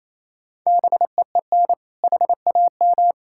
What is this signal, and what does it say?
Morse code, heard as a keyed tone, about 27 wpm, 700 Hz. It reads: BEEN HAM
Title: K1USN SST QSO Practice 28wpm